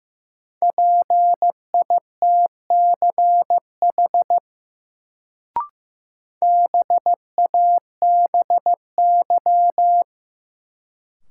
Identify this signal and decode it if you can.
Morse code, heard as a keyed tone, about 15 wpm, 700 Hz. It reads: PITCH BABY
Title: Sentences from Top 1300 Words - New Word Drill Alt - 15wpm